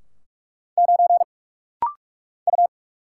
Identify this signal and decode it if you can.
Morse code, heard as a keyed tone, about 44 wpm, 700 Hz. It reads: 9 U
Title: Single Letter-Number Alt - Rapid-Fire 45wpm